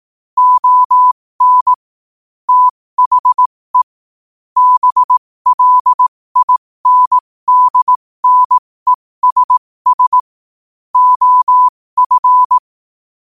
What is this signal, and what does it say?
Morse code, heard as a keyed tone, about 18 wpm, spaced as for 15 wpm, 1000 Hz. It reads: ON THE BLINDNESS OF